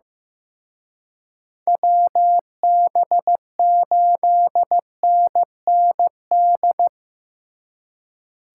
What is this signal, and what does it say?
Morse code, heard as a keyed tone, about 18 wpm, 700 Hz. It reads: WB8NND
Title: Speed Racing - USA Call Signs 15wpm